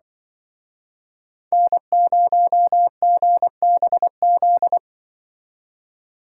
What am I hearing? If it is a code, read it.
Morse code, heard as a keyed tone, about 25 wpm, 700 Hz. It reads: N0GBZ
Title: Speed Racing - USA 1x3 Call Signs 20wpm